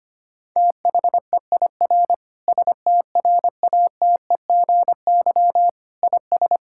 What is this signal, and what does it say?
Morse code, heard as a keyed tone, about 25 wpm, 700 Hz. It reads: THEIR STRATEGY IS
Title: Sentences from Top 1000 Words - Review Part 5 - 25wpm